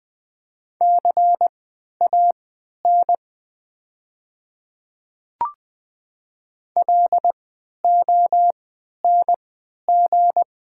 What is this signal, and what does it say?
Morse code, heard as a keyed tone, about 20 wpm, spaced as for 11 wpm, 700 Hz. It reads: CAN LONG
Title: 3x Character Spacing - Top 200 Words - 20wpm